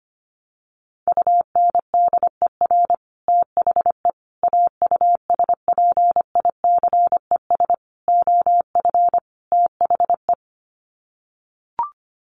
Morse code, heard as a keyed tone, about 25 wpm, 700 Hz. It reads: UNDER THE AUSPICES OF THE
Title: Sets of 5 Words Encore2 25wpm